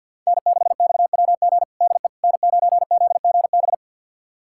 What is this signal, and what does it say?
Morse code, heard as a keyed tone, about 51 wpm, 700 Hz. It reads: N6XWG DE N9ZGB
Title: USA Callsign DE Callsign - 5 Characters - 50wpm